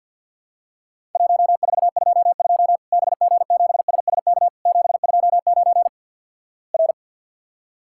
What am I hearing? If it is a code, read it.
Morse code, heard as a keyed tone, about 52 wpm, 700 Hz. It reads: 1422 BG7SSK 729 R
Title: Worked All Europe DX Contest - QTC Exchange with Repeats - 50wpm